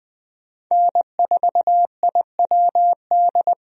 Morse code, heard as a keyed tone, about 20 wpm, 700 Hz. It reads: N4IWD
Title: USA Call Signs - 5 Characters - Repeated 3x - 20wpm